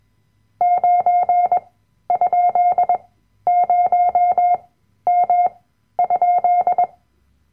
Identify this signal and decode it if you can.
Morse code, heard as a keyed tone, about 21 wpm, spaced as for 12 wpm, 710 Hz. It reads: 9?0M?
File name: Workshop Dateien / CW-Online-Kurs / Lektion 010